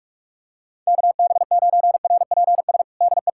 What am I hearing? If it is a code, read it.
Morse code, heard as a keyed tone, about 46 wpm, 700 Hz. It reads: KB9RPS DE
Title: USA Callsign DE Callsign - 6 Characters - 45wpm